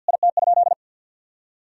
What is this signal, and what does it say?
Morse code, heard as a keyed tone, about 50 wpm, 700 Hz. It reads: IT?
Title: Sentences from Top 1000 Words - Review Part 4 - 50wpm